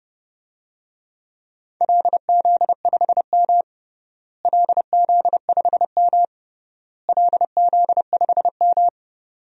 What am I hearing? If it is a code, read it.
Morse code, heard as a keyed tone, about 30 wpm, 700 Hz. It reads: LZ5M LZ5M LZ5M
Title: DXCC-100 Call Signs - 4 Characters - Repeated 3x - 30wpm